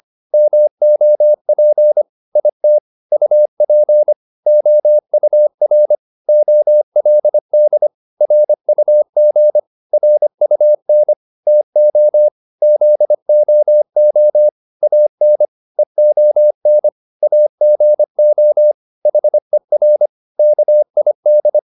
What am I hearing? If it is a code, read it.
Morse code, heard as a keyed tone, about 25 wpm, 600 Hz. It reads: MOP IT UP OUR OLD RUG RUN TO ZOO AN EON AGO HER KID